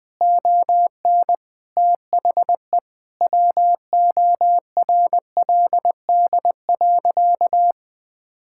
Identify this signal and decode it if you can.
Morse code, heard as a keyed tone, about 21 wpm, 700 Hz. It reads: ON THE WORLD.
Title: Sentences from Top 1000 Words - Review Part 6 - 20wpm